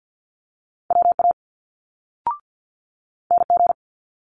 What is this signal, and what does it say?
Morse code, heard as a keyed tone, about 50 wpm, 700 Hz. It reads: WA NG
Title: Multi-letter Phonemes Quick Fire 50wpm